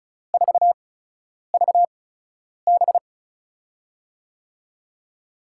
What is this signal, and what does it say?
Morse code, heard as a keyed tone, about 36 wpm, spaced as for 10 wpm, 700 Hz. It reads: 4VB
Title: USA Call Signs - 5 Characters - 8x Character Spacing - 35wpm